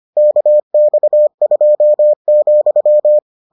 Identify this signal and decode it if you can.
Morse code, heard as a keyed tone, about 25 wpm, 600 Hz. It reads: KX2,